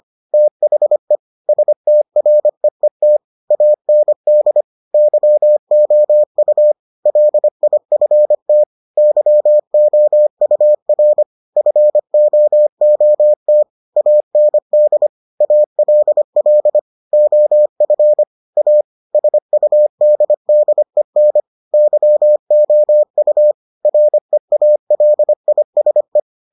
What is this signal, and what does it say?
Morse code, heard as a keyed tone, about 25 wpm, 600 Hz. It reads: THE STREET AND YOU LIFT YOUR FOOT AND ALL OF A SUDDEN YOU REALISE